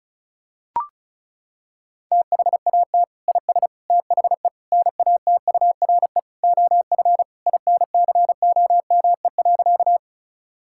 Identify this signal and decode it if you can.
Morse code, heard as a keyed tone, about 36 wpm, 700 Hz. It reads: THAT IS THE NATURE OF INCOME.
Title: Sentences from Top 1000 Words - Review Part 5 - 35wpm